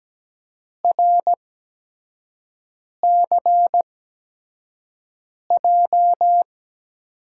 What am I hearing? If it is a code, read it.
Morse code, heard as a keyed tone, about 17 wpm, spaced as for 5 wpm, 700 Hz. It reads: RCJ
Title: USA Call Signs - 6 Characters - 8x Character Spacing - 17wpm